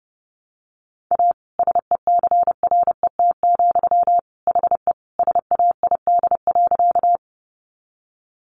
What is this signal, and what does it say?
Morse code, heard as a keyed tone, about 30 wpm, 700 Hz. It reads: A SECRET, HE SAID.
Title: Sentences from Top 1200 Words 30wpm